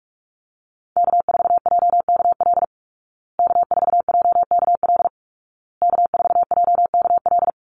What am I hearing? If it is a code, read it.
Morse code, heard as a keyed tone, about 44 wpm, 700 Hz. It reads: K4JKL K4JKL K4JKL
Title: Call Signs Repeated 2x - 45wpm